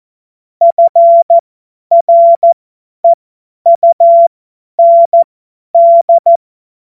Morse code, heard as a keyed tone, about 14 wpm, spaced as for 10 wpm, 680 Hz. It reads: FREUND
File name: Witz-23-September-2025-50-F70